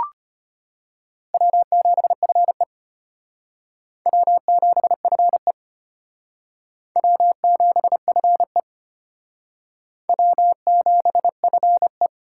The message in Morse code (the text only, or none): W7FE W7FE W7FE W7FE